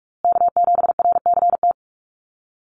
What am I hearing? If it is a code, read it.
Morse code, heard as a keyed tone, about 44 wpm, 700 Hz. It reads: K7RCT
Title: USA Call Signs Repeated 3x 45wpm